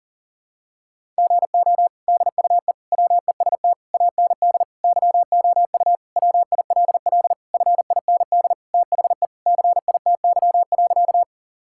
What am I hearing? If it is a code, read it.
Morse code, heard as a keyed tone, about 37 wpm, 700 Hz. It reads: GO DUE WEST AND YOU WILL FIND THE CITY.
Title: Sentences from Top 1000 Words - Review Part 6 - 40wpm